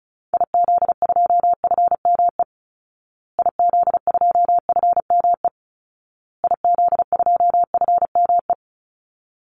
Morse code, heard as a keyed tone, about 34 wpm, 700 Hz. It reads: IZ2FME IZ2FME IZ2FME
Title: Call Signs Repeated 2x - 35wpm